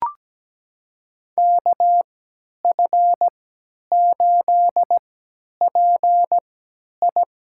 Morse code, heard as a keyed tone, about 17 wpm, spaced as for 10 wpm, 700 Hz. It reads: KF8PI
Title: USA Call Signs - 5 Characters - 3x Character Spacing - 17wpm